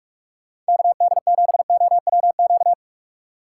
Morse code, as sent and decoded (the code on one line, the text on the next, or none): -.- -.. --... --- .-- --.-
KD7OWQ